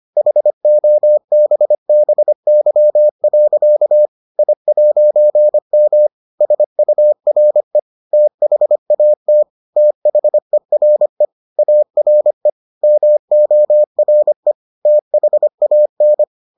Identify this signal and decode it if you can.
Morse code, heard as a keyed tone, about 25 wpm, 600 Hz. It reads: HOBBY. I'M SURE THAT THERE ARE MORE THAN